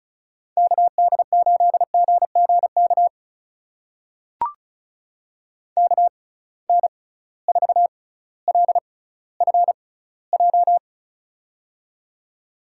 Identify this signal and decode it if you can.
Morse code, heard as a keyed tone, about 34 wpm, spaced as for 18 wpm, 700 Hz. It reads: KD8GGK KN4LFJ
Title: USA Call Signs - 6 Characters - 6x Character Spacing - 35wpm